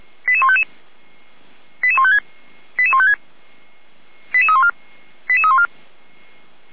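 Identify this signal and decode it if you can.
Five-tone selective call (ZVEI), heard as quick runs of five tones, tones of about 70 ms. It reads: 88155 88156 88156 88314 88314
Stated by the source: ZVEI five-tone selective call received on SDR.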